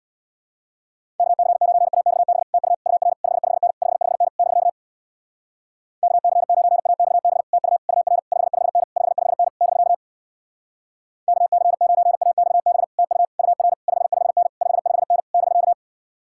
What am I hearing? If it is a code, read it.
Morse code, heard as a keyed tone, about 66 wpm, 700 Hz. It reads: KC9AXB TU UR 55N 55N <BK> KC9AXB TU UR 55N 55N <BK> KC9AXB TU UR 55N 55N <BK>